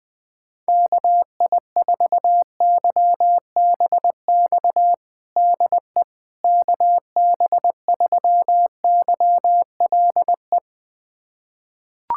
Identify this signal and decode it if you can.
Morse code, heard as a keyed tone, about 20 wpm, 700 Hz. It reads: KI4YBX DE KB3YLE
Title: USA Callsign DE Callsign - 6 Characters - 20wpm